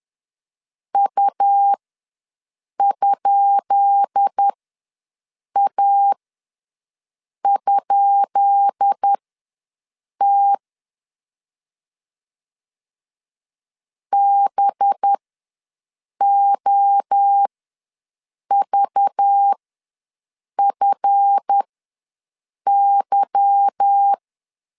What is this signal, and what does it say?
Morse code, heard as a keyed tone, about 10 wpm, spaced as for 6 wpm, 800 Hz. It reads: U?A?T BOVFY
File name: LEZIONI / 10 SIGNS / 10B INT ws 10wpm 4134